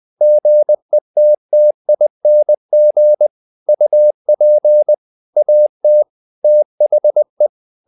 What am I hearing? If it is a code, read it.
Morse code, heard as a keyed tone, about 20 wpm, 600 Hz. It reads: GETTING UP AT THE
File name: the old man and the sea / the old man and the sea part 1 20wpm